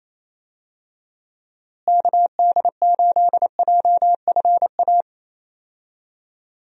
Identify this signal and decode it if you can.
Morse code, heard as a keyed tone, about 28 wpm, 700 Hz. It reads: KD8JFA
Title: USA Call Signs - 6 Characters - Repeated 3x - 28wpm